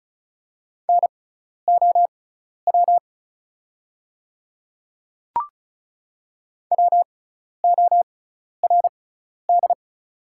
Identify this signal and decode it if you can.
Morse code, heard as a keyed tone, about 36 wpm, spaced as for 12 wpm, 700 Hz. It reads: NOW WORD
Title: 6x Character Spacing - Top 100 Words - 35wpm